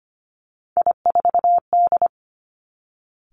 Morse code, heard as a keyed tone, about 25 wpm, 700 Hz. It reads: I4D
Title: Three Challenging Letter-Number Combinations 25wpm